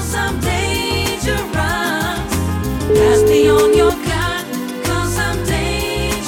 Ringback tone (Europe, 425 Hz), with noise.